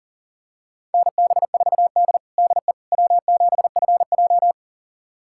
Morse code, heard as a keyed tone, about 40 wpm, 700 Hz. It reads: NB4D DE W7FJ
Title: USA Callsign DE Callsign - 4 Characters - 40wpm